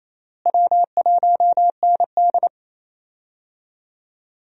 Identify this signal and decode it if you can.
Morse code, heard as a keyed tone, about 28 wpm, 700 Hz. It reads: W1ND
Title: USA Call Signs - 4 Characters - Repeated 2x - 28wpm